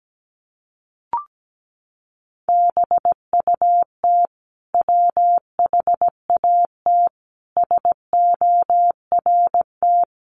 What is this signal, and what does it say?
Morse code, heard as a keyed tone, about 17 wpm, 700 Hz. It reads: BUT WHAT SORT
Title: Sentences from Top 1300 Words 17wpm